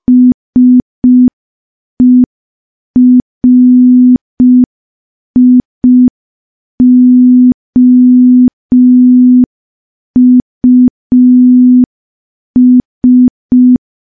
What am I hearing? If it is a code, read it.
Morse code, heard as a keyed tone, about 5 wpm, 250 Hz. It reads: SERIOUS
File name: morse / morse-training-05WPM